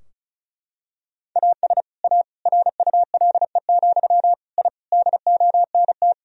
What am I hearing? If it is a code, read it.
Morse code, heard as a keyed tone, about 34 wpm, 700 Hz. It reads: AS A RULE, I DONT
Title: Sentences from Top 1000 Words - Review Part 2 - 35wpm